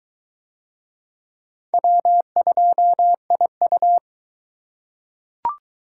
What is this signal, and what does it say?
Morse code, heard as a keyed tone, about 22 wpm, 700 Hz. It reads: W2IU